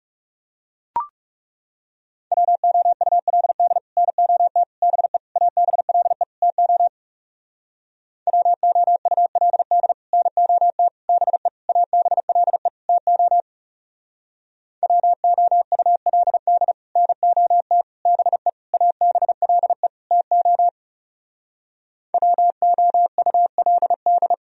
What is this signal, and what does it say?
Morse code, heard as a keyed tone, about 38 wpm, 700 Hz. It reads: WOULD NOT BE ABLE TO WOULD NOT BE ABLE TO WOULD NOT BE ABLE TO WOULD